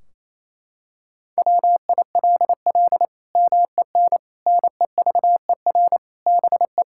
Morse code, heard as a keyed tone, about 28 wpm, 700 Hz. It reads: WILL MEN NEVER BE
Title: Sentences from Top 1600 Words 28wpm